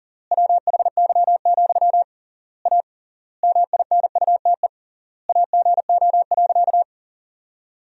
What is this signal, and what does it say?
Morse code, heard as a keyed tone, about 40 wpm, 700 Hz. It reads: WHY, A MINUTE AGO.
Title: 3x Word-Spacing - Sentences from Top 500 Words - 40wpm